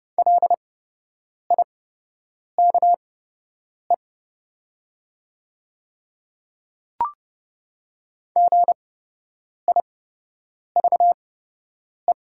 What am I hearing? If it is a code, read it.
Morse code, heard as a keyed tone, about 30 wpm, spaced as for 8 wpm, 700 Hz. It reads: LIKE GIVE
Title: 8x Character Spacing - Top 500 Words - 30wpm